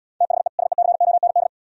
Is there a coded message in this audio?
Yes: Morse code from 0.2 to 1.5 s, heard as a keyed tone, about 76 wpm, 700 Hz. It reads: THE SECOND